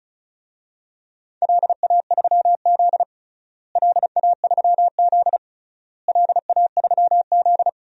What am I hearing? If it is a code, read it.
Morse code, heard as a keyed tone, about 36 wpm, 700 Hz. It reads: LA3Z LA3Z LA3Z
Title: DXCC-100 Call Signs - 4 Characters - Repeated 3x - 35wpm